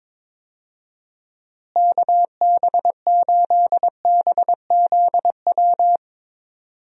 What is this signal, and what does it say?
Morse code, heard as a keyed tone, about 22 wpm, 700 Hz. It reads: KB8BZW
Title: USA Call Signs - 6 Characters - Repeated 3x - 22wpm